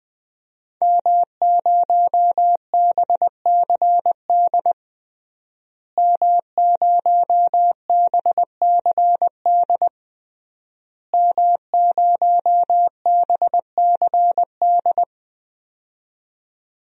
Morse code, heard as a keyed tone, about 21 wpm, 700 Hz. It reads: M0BCD M0BCD M0BCD
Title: DXCC-100 Call Signs - 5 Characters - Repeated 3x - 20wpm